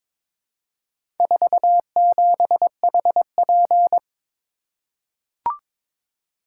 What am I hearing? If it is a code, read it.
Morse code, heard as a keyed tone, about 22 wpm, 700 Hz. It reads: 47HP
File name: practice / 22wpm / Four Letter-Number Combinations - 22wpm